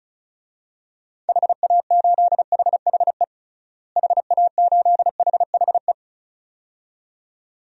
Morse code, heard as a keyed tone, about 36 wpm, 700 Hz. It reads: HA8HHE HA8HHE
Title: DXCC-100 Call Signs - 6 Characters - Repeated 2x - 35wpm